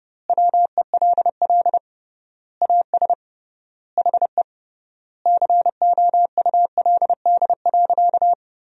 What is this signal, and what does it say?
Morse code, heard as a keyed tone, about 30 wpm, 700 Hz. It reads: WELL AS HE COULD.
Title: 3x Word-Spacing - Sentences from Top 500 Words - 30wpm